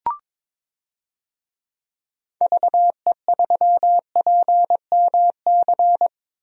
Morse code, heard as a keyed tone, about 22 wpm, 700 Hz. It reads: VE3PMC
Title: DXCC-100 Call Signs - 6 Characters - Repeated 3x - 22wpm